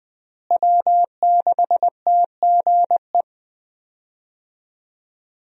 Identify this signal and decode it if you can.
Morse code, heard as a keyed tone, about 20 wpm, 700 Hz. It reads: W6TGE
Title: USA Call Signs - 5 Characters - Repeated 3x - 20wpm